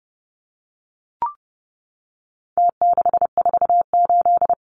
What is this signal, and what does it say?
Morse code, heard as a keyed tone, about 28 wpm, 700 Hz. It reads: T648
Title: Numbers - Contest Style - 4-Digit 30wpm